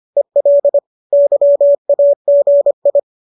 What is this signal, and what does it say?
Morse code, heard as a keyed tone, about 25 wpm, 570 Hz. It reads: EL YAGI